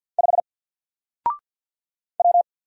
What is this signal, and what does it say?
Morse code, heard as a keyed tone, about 48 wpm, 700 Hz. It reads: 5 W
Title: Single Character (Letter:Number:Punctuation:Prosign) Alt 50wpm